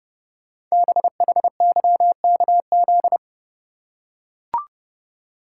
Morse code, heard as a keyed tone, about 29 wpm, 700 Hz. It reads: BHYKZ
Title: Code Groups - 5 Letters - 30wpm